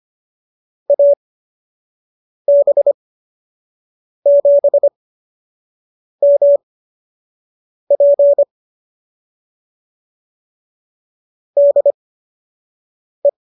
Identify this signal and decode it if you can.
Morse code, heard as a keyed tone, about 25 wpm, spaced as for 6 wpm, 570 Hz. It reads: AB7MP DE